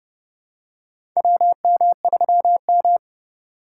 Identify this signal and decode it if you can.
Morse code, heard as a keyed tone, about 30 wpm, 700 Hz. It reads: WM3M
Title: USA Call Signs - 4 Characters - 3x Character Spacing - 30wpm